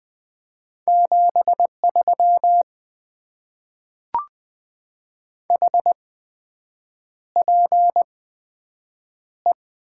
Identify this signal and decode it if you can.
Morse code, heard as a keyed tone, about 20 wpm, spaced as for 7 wpm, 700 Hz. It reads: 73 HPE
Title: Top 90 QSO Elements - 8x Character Spacing - 20wpm